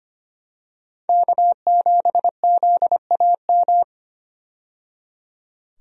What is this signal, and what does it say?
Morse code, heard as a keyed tone, about 25 wpm, 690 Hz. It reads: K7ZAM
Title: USA Call Signs - 5 Characters - 25wpm